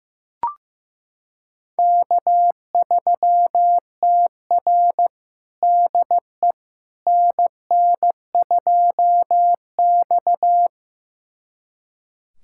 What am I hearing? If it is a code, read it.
Morse code, heard as a keyed tone, about 15 wpm, 700 Hz. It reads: K3TR DE NN2X
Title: USA Callsign DE Callsign - 4 Characters - 15wpm